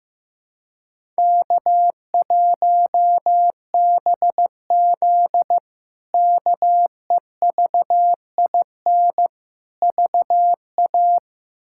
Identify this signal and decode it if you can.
Morse code, heard as a keyed tone, about 15 wpm, 700 Hz. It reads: K1BZ KEVIN VA
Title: K1USN SST QSO Practice - 15wpm